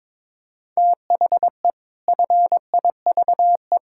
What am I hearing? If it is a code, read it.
Morse code, heard as a keyed tone, about 21 wpm, 700 Hz. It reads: THE FIVE
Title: Pangram Challenge v2 - 22wpm